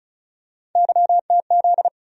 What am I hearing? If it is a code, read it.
Morse code, heard as a keyed tone, about 36 wpm, 700 Hz. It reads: YTZ
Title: USA Call Sign Postfixes - 3 Letters 35wpm